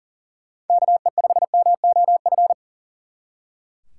Morse code, heard as a keyed tone, about 49 wpm, 700 Hz. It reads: KE5MOF
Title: Speed Racing - USA Call Signs 40wpm